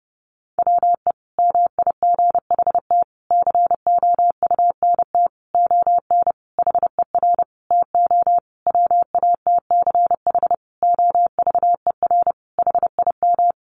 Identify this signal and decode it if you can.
Morse code, heard as a keyed tone, about 30 wpm, 700 Hz. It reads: WE MIGHT COUNT ON HER TO WATCH OVER HIM
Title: Sentences from Top 1500 Words 30wpm